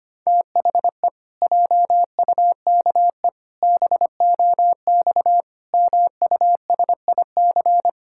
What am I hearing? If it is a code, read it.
Morse code, heard as a keyed tone, about 25 wpm, 700 Hz. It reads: THE JUKE BOX MUSIC
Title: Pangram Challenge v2 - 25wpm